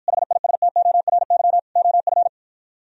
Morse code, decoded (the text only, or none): HISTORY OF